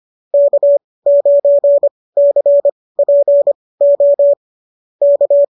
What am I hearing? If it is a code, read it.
Morse code, heard as a keyed tone, about 25 wpm, spaced as for 18 wpm, 590 Hz. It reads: K9CPO K